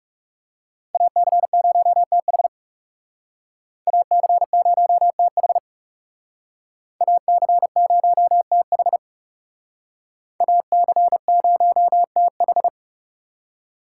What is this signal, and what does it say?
Morse code, heard as a keyed tone, about 39 wpm, 700 Hz. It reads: AC0TH AC0TH AC0TH AC0TH